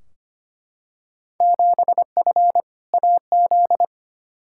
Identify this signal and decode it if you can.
Morse code, heard as a keyed tone, about 25 wpm, 700 Hz. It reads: 7F AZ
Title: Field Day Exchanges Encore - 25wpm